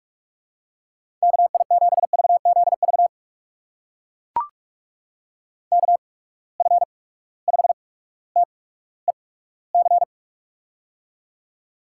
Morse code, heard as a keyed tone, about 44 wpm, spaced as for 19 wpm, 700 Hz. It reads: KI7VZV KF5TEC
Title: USA Call Signs - 6 Characters - 8x Character Spacing - 45wpm